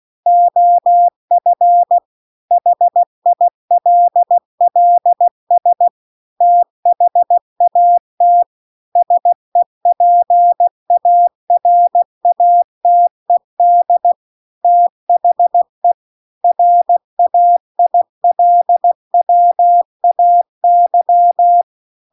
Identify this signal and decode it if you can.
Morse code, heard as a keyed tone, about 16 wpm, 700 Hz. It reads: OF HILLS THAT SEPARATED THE RAILWAY